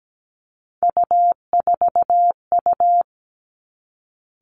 Morse code, heard as a keyed tone, about 17 wpm, 700 Hz. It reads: U4U
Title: Three Challenging Letter-Number Combinations 17wpm